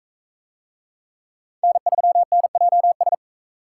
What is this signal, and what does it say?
Morse code, heard as a keyed tone, about 38 wpm, 700 Hz. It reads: N3NJS